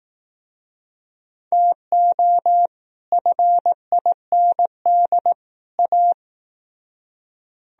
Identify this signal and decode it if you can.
Morse code, heard as a keyed tone, about 17 wpm, 700 Hz. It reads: TO FIND A
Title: Speed-Racing - Sets of 3 Words 15wpm